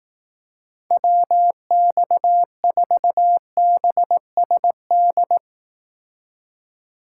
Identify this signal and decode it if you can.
Morse code, heard as a keyed tone, about 18 wpm, 700 Hz. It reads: WX4BSD
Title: Speed Racing - USA 2x3 Call Signs 15wpm